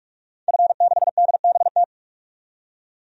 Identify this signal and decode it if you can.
Morse code, heard as a keyed tone, about 46 wpm, 700 Hz. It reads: F6DBT
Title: DXCC-100 Call Signs - 5 Characters - 45wpm